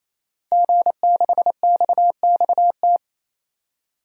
Morse code, heard as a keyed tone, about 28 wpm, 700 Hz. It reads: G6XXT
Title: DXCC-100 Call Signs - 5 Characters - 28wpm